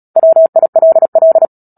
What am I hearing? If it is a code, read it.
Morse code, heard as a keyed tone, about 37 wpm, 650 Hz. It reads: WILL
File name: Book Two / 10-Chapter 10 - Epilogue - 36 WPM